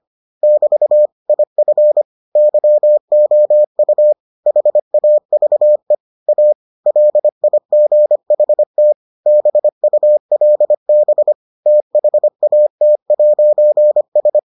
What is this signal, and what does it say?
Morse code, heard as a keyed tone, about 25 wpm, 600 Hz. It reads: <BT> IF YOU HAVE A LIGHT BULB THAT'S